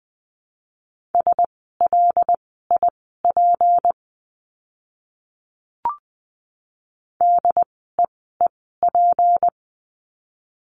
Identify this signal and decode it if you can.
Morse code, heard as a keyed tone, about 20 wpm, spaced as for 14 wpm, 700 Hz. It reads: SLIP DEEP
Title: 153 Lesson 25f - Letter p - Words 20wpm